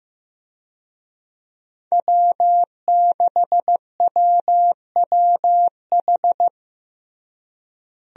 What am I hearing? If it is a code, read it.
Morse code, heard as a keyed tone, about 15 wpm, 700 Hz. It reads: W6WWH
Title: USA Call Signs - 5 Characters - Repeated 3x - 15wpm